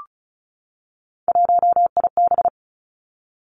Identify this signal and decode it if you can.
Morse code, heard as a keyed tone, about 36 wpm, 700 Hz. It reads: 1IB
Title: Three Challenging Letter-Number Combinations 35wpm